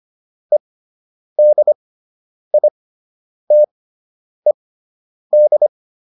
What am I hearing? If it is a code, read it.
Morse code, heard as a keyed tone, about 25 wpm, spaced as for 9 wpm, 600 Hz. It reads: EDITED